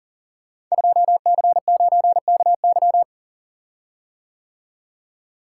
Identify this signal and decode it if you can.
Morse code, heard as a keyed tone, about 40 wpm, 700 Hz. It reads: 2C9KY
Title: Code Groups - 5 Letter:Numbers - 40wpm